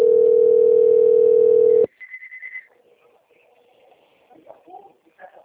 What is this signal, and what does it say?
Ringback tone (North America), 440 + 480 Hz, with speech.
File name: rg-1600-unknown- - - .7074